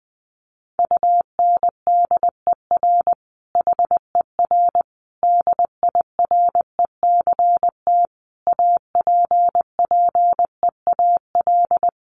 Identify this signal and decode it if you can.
Morse code, heard as a keyed tone, about 20 wpm, 700 Hz. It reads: UNDER HER DIRECT APPEAL
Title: Sentences from Top 1300 Words 20wpm